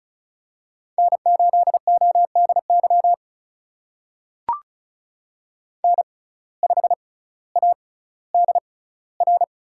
Morse code, heard as a keyed tone, about 36 wpm, spaced as for 18 wpm, 700 Hz. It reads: N8ODY N5ADR
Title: USA Call Signs - 5 Characters - 6x Character Spacing - 35wpm